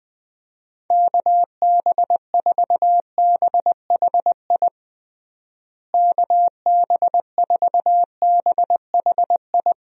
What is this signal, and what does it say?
Morse code, heard as a keyed tone, about 20 wpm, 700 Hz. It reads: KB4BHI KB4BHI
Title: USA Call Signs - 6 Characters - Repeated 2x - 20wpm